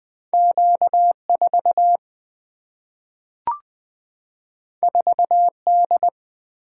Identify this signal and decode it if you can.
Morse code, heard as a keyed tone, about 20 wpm, 700 Hz. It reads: Q 4 4 D
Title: Single-Letter Single-Number 20wpm